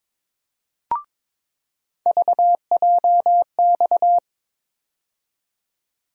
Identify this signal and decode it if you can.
Morse code, heard as a keyed tone, about 22 wpm, 700 Hz. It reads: VJX
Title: USA Call Sign Postfixes - 3 Letters 22wpm